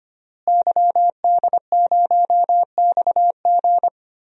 Morse code, heard as a keyed tone, about 25 wpm, 700 Hz. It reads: YD0XG
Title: DXCC-100 Call Signs - 5 Characters - Repeated 2x - 25wpm